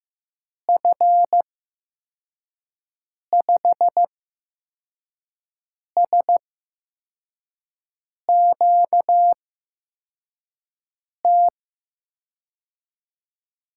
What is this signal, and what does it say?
Morse code, heard as a keyed tone, about 15 wpm, spaced as for 4 wpm, 700 Hz. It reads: F5SQT
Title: USA Call Signs - 8x Character Spacing - 15wpm